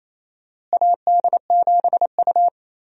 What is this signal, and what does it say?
Morse code, heard as a keyed tone, about 28 wpm, 700 Hz. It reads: AD7U
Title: USA Call Signs - 4 Characters - Repeated 3x - 28wpm